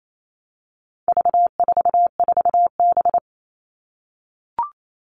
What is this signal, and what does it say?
Morse code, heard as a keyed tone, about 28 wpm, 700 Hz. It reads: V44B
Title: All Cut Numbers - 4-Digit 28wpm